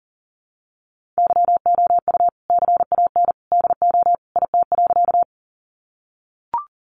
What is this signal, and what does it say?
Morse code, heard as a keyed tone, about 40 wpm, 700 Hz. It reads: YOU CAN DO IT.
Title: Sentences from Top 100 Words 40wpm